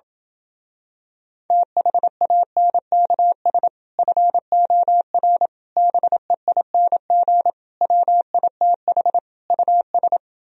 Morse code, heard as a keyed tone, about 27 wpm, 700 Hz. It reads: THANKS FOR BEING WITH US